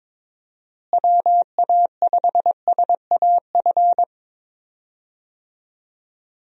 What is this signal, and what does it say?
Morse code, heard as a keyed tone, about 22 wpm, 700 Hz. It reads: WA5SAF
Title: USA Call Signs - 6 Characters - Repeated 2x - 22wpm